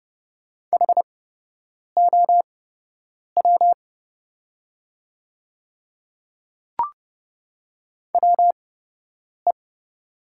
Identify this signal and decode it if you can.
Morse code, heard as a keyed tone, about 30 wpm, spaced as for 8 wpm, 700 Hz. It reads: HOW WE
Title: 8x Character Spacing - Top 200 Words - 30wpm